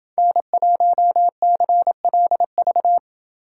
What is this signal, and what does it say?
Morse code, heard as a keyed tone, about 28 wpm, 700 Hz. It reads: N1CLV